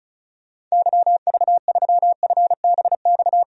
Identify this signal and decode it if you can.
Morse code, heard as a keyed tone, about 36 wpm, 700 Hz. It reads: YV3FBX
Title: DXCC-100 Call Signs - 6 Characters - Repeated 3x - 35wpm